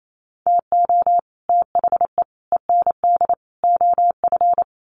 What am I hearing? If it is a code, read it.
Morse code, heard as a keyed tone, about 28 wpm, 700 Hz. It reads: TO THE END OF
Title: Sets of 4 Words 28wpm